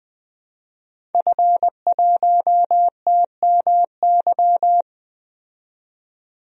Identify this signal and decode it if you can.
Morse code, heard as a keyed tone, about 21 wpm, 700 Hz. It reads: F1TMY
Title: DXCC-100 Call Signs - 5 Characters - 20wpm